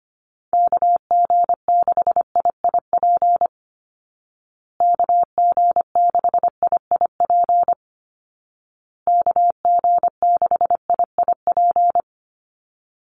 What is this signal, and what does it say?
Morse code, heard as a keyed tone, about 25 wpm, 700 Hz. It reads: KG6IIP KG6IIP KG6IIP
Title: Call Signs Repeated 2x - 25wpm